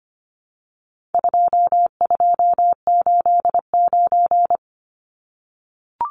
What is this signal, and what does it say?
Morse code, heard as a keyed tone, about 25 wpm, 700 Hz. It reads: 2289
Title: Numbers - 4-Digit 25wpm